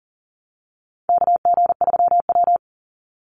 Morse code, heard as a keyed tone, about 40 wpm, 700 Hz. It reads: KG3W
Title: NA SOTA Chaser Callsigns 40wpm